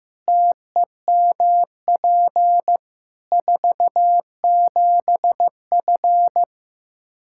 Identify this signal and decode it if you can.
Morse code, heard as a keyed tone, about 15 wpm, 700 Hz. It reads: TEMP 47F
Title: Practice QSO - Exchange 2 - Info - 15wpm